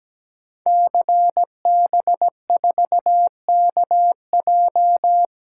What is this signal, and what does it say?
Morse code, heard as a keyed tone, about 17 wpm, 700 Hz. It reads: CB4KJ
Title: Code Groups - 5 Letter:Numbers - 17wpm